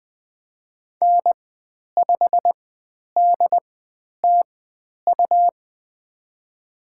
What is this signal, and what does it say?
Morse code, heard as a keyed tone, about 20 wpm, spaced as for 10 wpm, 700 Hz. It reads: N5DTU
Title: 102 Lesson 17g - Letter u - Call Signs 20wpm